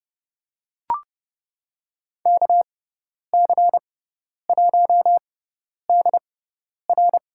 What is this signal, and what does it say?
Morse code, heard as a keyed tone, about 29 wpm, spaced as for 10 wpm, 700 Hz. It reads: KC1DR
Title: USA Call Signs - 6x Character Spacing - 30wpm